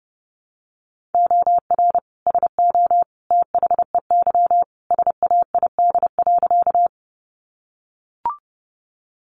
Morse code, heard as a keyed tone, about 30 wpm, 700 Hz. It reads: OR SO THEY SAID.
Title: Sentences from Top 100 Words 30wpm